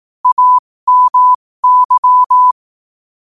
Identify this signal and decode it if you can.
Morse code, heard as a keyed tone, about 18 wpm, spaced as for 15 wpm, 1000 Hz. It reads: AMY
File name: qotdfiles / 15wpm